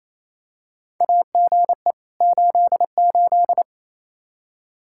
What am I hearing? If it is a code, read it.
Morse code, heard as a keyed tone, about 28 wpm, 700 Hz. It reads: AGE 88
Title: Practice QSO - Exchange 3 - More Info - 28wpm